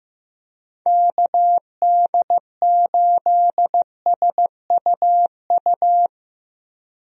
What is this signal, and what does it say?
Morse code, heard as a keyed tone, about 16 wpm, 700 Hz. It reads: KD8SUU
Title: USA Call Signs - 6 Characters - Repeated 2x - 15wpm